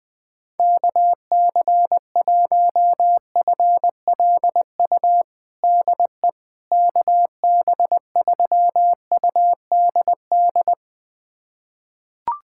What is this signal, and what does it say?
Morse code, heard as a keyed tone, about 20 wpm, 700 Hz. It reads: KC1FLU DE KB3UDD
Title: USA Callsign DE Callsign - 6 Characters - 20wpm